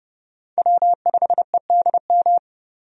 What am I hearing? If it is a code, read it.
Morse code, heard as a keyed tone, about 30 wpm, 700 Hz. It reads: W5EDM
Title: USA Call Signs - 5 Characters - 30wpm